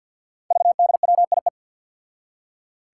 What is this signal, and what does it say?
Morse code, heard as a keyed tone, about 53 wpm, 700 Hz. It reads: VDWIE
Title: Five Challenging Letter-Number Combinations 50wpm